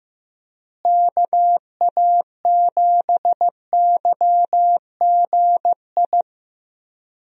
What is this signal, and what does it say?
Morse code, heard as a keyed tone, about 15 wpm, 700 Hz. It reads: KA7YGI
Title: USA Call Signs - 6 Characters - Repeated 3x - 15wpm